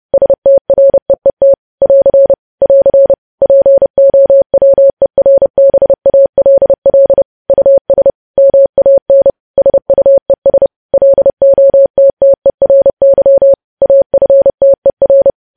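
Morse code, heard as a keyed tone, about 30 wpm, 550 Hz. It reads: STREET <AR> <AR> POWERBALL US MAN SUES LOTTERY AFTER
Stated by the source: Morse news headlines sent at 30 words per minute.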